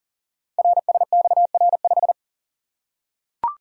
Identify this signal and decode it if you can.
Morse code, heard as a keyed tone, about 39 wpm, 690 Hz. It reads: RSXR5
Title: Code Groups - 5 Letter:Numbers - 40wpm